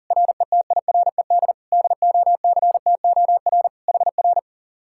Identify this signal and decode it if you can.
Morse code, heard as a keyed tone, about 50 wpm, 700 Hz. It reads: RETIRED DOCTOR HR
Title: Speed-Racing Common QSO Phrases - 40wpm